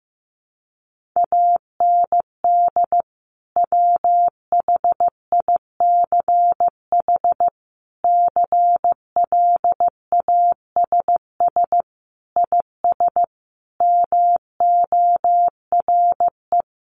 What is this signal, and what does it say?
Morse code, heard as a keyed tone, about 15 wpm, 700 Hz. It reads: AND WHICH CLASS IS MORE
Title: Sentences from Top 1400 Words 15wpm